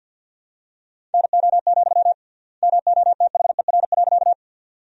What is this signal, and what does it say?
Morse code, heard as a keyed tone, about 47 wpm, 700 Hz. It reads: NO, MOTHER.
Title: 3x Word-Spacing - Sentences from Top 200 Words - 50wpm